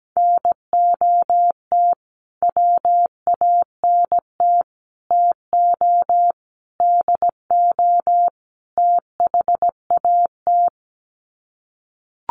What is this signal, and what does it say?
Morse code, heard as a keyed tone, about 17 wpm, 700 Hz. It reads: NOT WANT TO DO THAT
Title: Sets of 5 Words Encore 17wpm